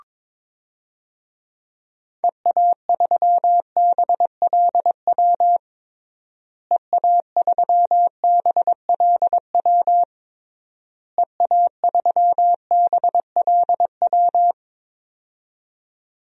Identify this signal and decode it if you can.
Morse code, heard as a keyed tone, about 22 wpm, 700 Hz. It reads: EA3BLW EA3BLW EA3BLW
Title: DXCC-100 Call Signs - 6 Characters - Repeated 3x - 22wpm